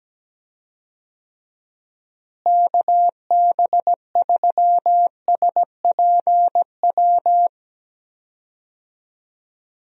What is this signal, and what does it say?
Morse code, heard as a keyed tone, about 17 wpm, 700 Hz. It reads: KB3SPW
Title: USA Call Signs - 6 Characters - Repeated 3x - 17wpm